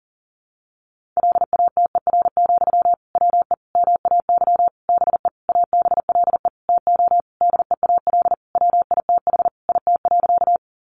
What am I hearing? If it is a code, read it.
Morse code, heard as a keyed tone, about 40 wpm, 700 Hz. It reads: LATER, WE MAY BE ABLE TO DEAL WITH IT.
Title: Sentences from Top 800 Words 40wpm